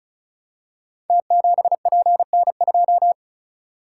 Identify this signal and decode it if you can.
Morse code, heard as a keyed tone, about 36 wpm, 700 Hz. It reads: T7PN2
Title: Code Groups - 5 Letter:Numbers - 35wpm